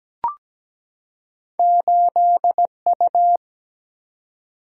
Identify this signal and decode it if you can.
Morse code, heard as a keyed tone, about 17 wpm, 700 Hz. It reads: 8U
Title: Single-Letter Single-Number 17wpm